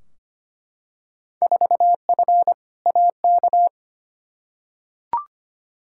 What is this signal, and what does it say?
Morse code, heard as a keyed tone, about 25 wpm, 700 Hz. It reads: 4F AK
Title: Field Day Exchanges Encore - 25wpm